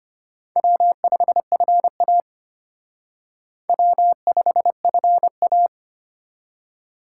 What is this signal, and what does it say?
Morse code, heard as a keyed tone, about 32 wpm, 700 Hz. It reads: W5FA W5FA